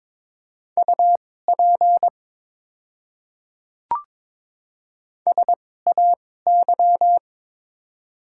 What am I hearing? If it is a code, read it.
Morse code, heard as a keyed tone, about 22 wpm, spaced as for 16 wpm, 690 Hz. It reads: UP SAY
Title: 2x Character Spacing - Top 300 Words - 22wpm